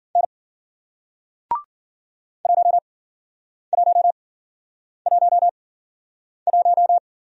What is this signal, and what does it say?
Morse code, heard as a keyed tone, about 54 wpm, spaced as for 9 wpm, 700 Hz. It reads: N 1111